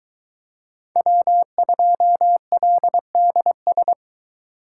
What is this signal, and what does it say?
Morse code, heard as a keyed tone, about 22 wpm, 700 Hz. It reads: W2LDS